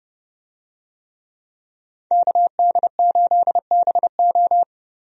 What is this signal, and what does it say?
Morse code, heard as a keyed tone, about 30 wpm, 700 Hz. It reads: KD8BO
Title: USA Call Signs - 5 Characters - 30wpm